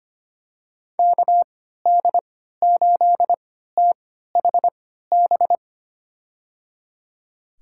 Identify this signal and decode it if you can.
Morse code, heard as a keyed tone, about 25 wpm, spaced as for 14 wpm, 690 Hz. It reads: KD8THB
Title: USA Call Signs - 6 Characters - 3x Character Spacing - 25wpm